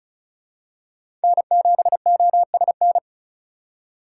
Morse code, heard as a keyed tone, about 34 wpm, 700 Hz. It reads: N7OSN
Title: USA Call Signs - 5 Characters - 3x Character Spacing - 35wpm